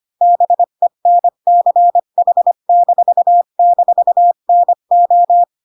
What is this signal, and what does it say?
Morse code, heard as a keyed tone, about 25 wpm, 700 Hz. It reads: BENCH--NO
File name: Pinocchio / Pinocchio ch1 20 25 ik1hge